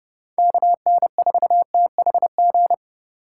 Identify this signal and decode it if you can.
Morse code, heard as a keyed tone, about 30 wpm, 700 Hz. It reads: KN4THG
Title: USA Call Signs - 6 Characters - 3x Character Spacing - 30wpm